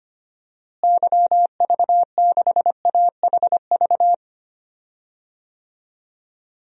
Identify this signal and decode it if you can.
Morse code, heard as a keyed tone, about 25 wpm, 700 Hz. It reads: YV6AHV
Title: DXCC-100 Call Signs - 6 Characters - Repeated 2x - 25wpm